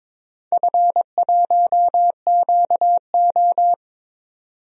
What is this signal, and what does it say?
Morse code, heard as a keyed tone, about 22 wpm, 700 Hz. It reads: F1QO
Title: DXCC-100 Call Signs - 4 Characters - 22wpm